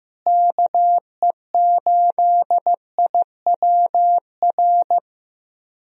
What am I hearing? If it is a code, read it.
Morse code, heard as a keyed tone, about 15 wpm, 700 Hz. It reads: KE8IWR
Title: USA Call Signs - 6 Characters - Repeated 2x - 15wpm